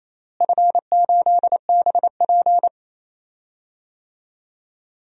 Morse code, heard as a keyed tone, about 28 wpm, 700 Hz. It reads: F8BP
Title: DXCC-100 Call Signs - 4 Characters - Repeated 2x - 28wpm